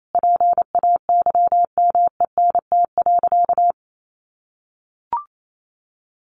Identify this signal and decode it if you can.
Morse code, heard as a keyed tone, about 28 wpm, 700 Hz. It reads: PAYMENT.
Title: Sentences from Top 1400 Words 28wpm